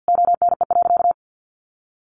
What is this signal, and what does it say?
Morse code, heard as a keyed tone, about 48 wpm, 700 Hz. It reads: ONE.
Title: Sentences from Top 1500 Words 50wpm